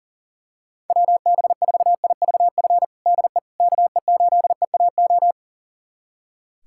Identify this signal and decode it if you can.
Morse code, heard as a keyed tone, about 37 wpm, 700 Hz. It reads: WB4IVF DE KE8EAO
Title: USA Callsign DE Callsign 40wpm